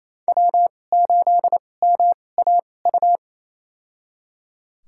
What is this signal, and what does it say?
Morse code, heard as a keyed tone, about 28 wpm, spaced as for 20 wpm, 700 Hz. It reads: W8MAU
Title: USA Call Signs - 5 Characters - 2x Character Spacing - 28wpm